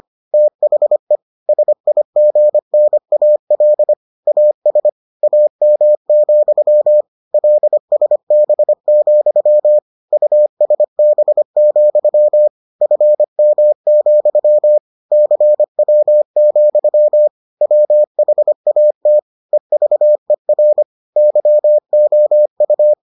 Morse code, heard as a keyed tone, about 25 wpm, 600 Hz. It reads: THE SIGNAL AS AM, LSB, USB, FM, CW, WHAT EVER YOU